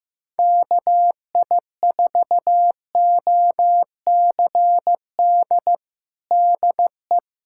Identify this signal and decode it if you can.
Morse code, heard as a keyed tone, about 15 wpm, 700 Hz. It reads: KI4OCD DE
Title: USA Callsign DE Callsign - 6 Characters - 15wpm